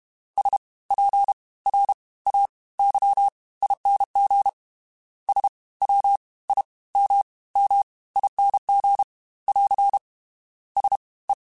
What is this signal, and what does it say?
Morse code, heard as a keyed tone, about 32 wpm, 800 Hz. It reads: S P R A Y ING S W I M M ING <AR> S E
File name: Special INGs / Special INGs-mp3s / Fast ing @30